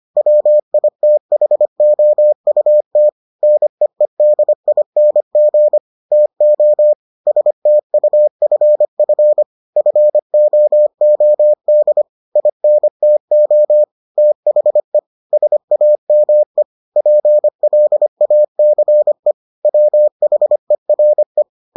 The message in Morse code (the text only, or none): WITHOUT NEEDING TO STUFF FOOD INTO THE SAME PLACE WHERE